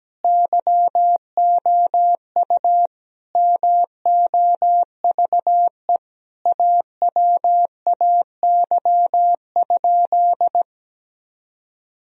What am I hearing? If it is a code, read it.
Morse code, heard as a keyed tone, about 17 wpm, 700 Hz. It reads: YOU MOVE AWAY?
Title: Sentences from Top 1000 Words - Review Part 1 - 17wpm